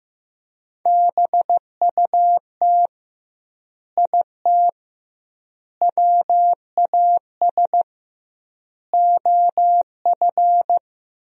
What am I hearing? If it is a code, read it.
Morse code, heard as a keyed tone, about 15 wpm, 700 Hz. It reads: BUT IT WAS OF
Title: 2x Word-Spacing - Sentences from Top 100 Words - 15wpm